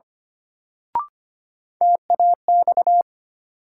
Morse code, heard as a keyed tone, about 25 wpm, 700 Hz. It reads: TAX
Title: Sentences from Top 1000 Words - Review Part 5 - New Word Drill - 25wpm